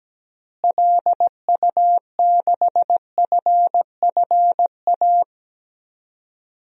Morse code, heard as a keyed tone, about 17 wpm, 700 Hz. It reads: LU6FFA
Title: DXCC-100 Call Signs - 6 Characters - Repeated 3x - 17wpm